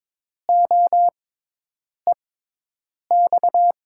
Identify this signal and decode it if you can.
Morse code, heard as a keyed tone, about 22 wpm, spaced as for 8 wpm, 700 Hz. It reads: OEX
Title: USA Call Signs - 6 Characters - 6x Character Spacing - 22wpm